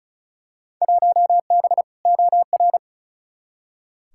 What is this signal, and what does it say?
Morse code, heard as a keyed tone, about 36 wpm, 700 Hz. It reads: 1B OR
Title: Field Day Exchanges - 35wpm